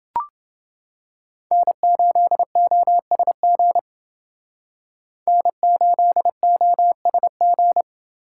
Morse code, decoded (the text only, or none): N8OSG N8OSG